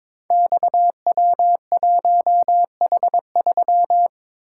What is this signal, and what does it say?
Morse code, heard as a keyed tone, about 22 wpm, 700 Hz. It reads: XW1H3
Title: Code Groups - 5 Letter:Numbers - 22wpm